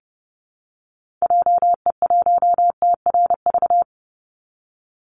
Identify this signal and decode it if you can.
Morse code, heard as a keyed tone, about 29 wpm, 700 Hz. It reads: JE1TRV
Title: Call Signs Repeated 2x - 30wpm